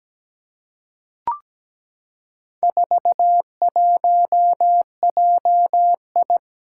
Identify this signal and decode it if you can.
Morse code, heard as a keyed tone, about 17 wpm, 700 Hz. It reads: 41JI
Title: Four Challenging Letter-Number Combinations 17wpm